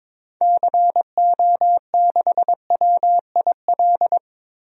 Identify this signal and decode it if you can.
Morse code, heard as a keyed tone, about 22 wpm, 700 Hz. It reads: CO6WIL
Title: DXCC-100 Call Signs - 6 Characters - Repeated 2x - 22wpm